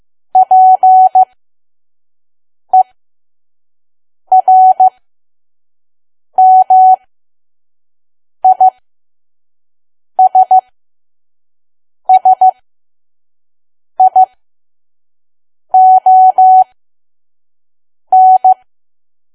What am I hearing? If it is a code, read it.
Morse code, heard as a keyed tone, about 15 wpm, spaced as for 5 wpm, 750 Hz. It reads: PERMISSION
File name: Archive / 5 WPM files / PM